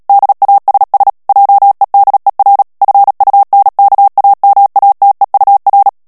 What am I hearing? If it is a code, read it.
Morse code, heard as a keyed tone, about 37 wpm, 800 Hz. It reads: DASS JEDER FUNKAMATEUR
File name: morse mp3 / ALLEMAND texte clair 2 vitesse 35 mots minute